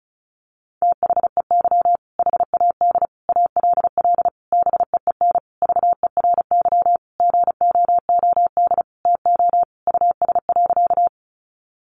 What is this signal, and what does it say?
Morse code, heard as a keyed tone, about 34 wpm, 700 Hz. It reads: THEY HAD ALL BEEN VERY GOOD TO US.
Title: Sentences from Top 200 Words 35wpm